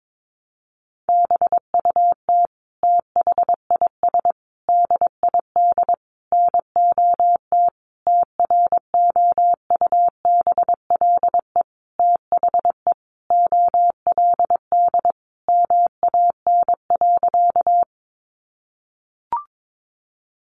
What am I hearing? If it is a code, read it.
Morse code, heard as a keyed tone, about 22 wpm, 700 Hz. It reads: BUT THIS DID NOT TROUBLE THE OLD MAN.
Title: Sentences from Top 1200 Words 22wpm